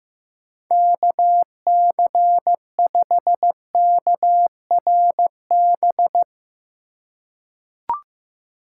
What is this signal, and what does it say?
Morse code, heard as a keyed tone, about 15 wpm, 700 Hz. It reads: KC5KRB
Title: USA Call Signs - 6 Characters - 2x Character Spacing - 15wpm